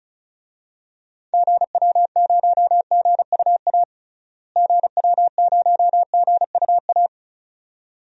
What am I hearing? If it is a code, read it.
Morse code, heard as a keyed tone, about 36 wpm, 700 Hz. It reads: GW0GUA GW0GUA
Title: DXCC-100 Call Signs - 6 Characters - Repeated 2x - 35wpm